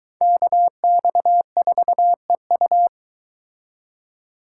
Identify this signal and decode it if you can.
Morse code, heard as a keyed tone, about 23 wpm, 700 Hz. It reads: KX4EU